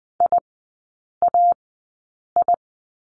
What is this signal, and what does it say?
Morse code, heard as a keyed tone, about 20 wpm, spaced as for 8 wpm, 700 Hz. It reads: IAI
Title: 041 Lesson 8e - Number 1 - Three Characters 20wpm